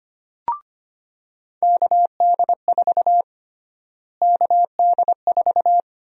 Morse code, heard as a keyed tone, about 25 wpm, 700 Hz. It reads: KD4 KD4
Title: USA Call Sign Prefixes - Repeated 2x - 25wpm